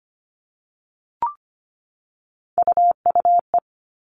Morse code, heard as a keyed tone, about 25 wpm, 700 Hz. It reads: UUE
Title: All Cut Numbers - 3-Digit 25wpm